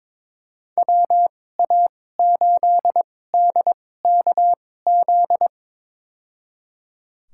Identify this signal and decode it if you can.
Morse code, heard as a keyed tone, about 22 wpm, spaced as for 16 wpm, 700 Hz. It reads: WA8DKZ
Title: USA Call Signs - 6 Characters - 2x Character Spacing - 22wpm